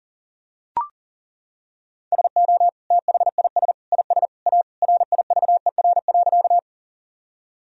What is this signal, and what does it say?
Morse code, heard as a keyed tone, about 37 wpm, 700 Hz. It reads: SO THIS IS A RIVER.
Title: Sentences from Top 1000 Words - Review Part 1 - 40wpm